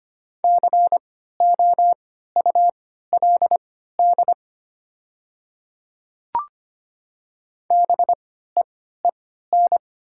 Morse code, heard as a keyed tone, about 25 wpm, spaced as for 14 wpm, 700 Hz. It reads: COULD BEEN
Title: 3x Character Spacing - Top 100 Words - 25wpm